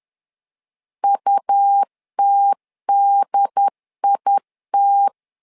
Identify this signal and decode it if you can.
Morse code, heard as a keyed tone, about 11 wpm, 800 Hz. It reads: UTDIT
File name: LEZIONI / 3 MWRO / 3 M 10wpm 1828